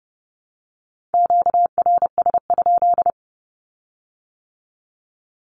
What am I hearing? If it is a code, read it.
Morse code, heard as a keyed tone, about 32 wpm, 700 Hz. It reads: QRS?
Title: Q-Codes with Spoken Meaning - 30wpm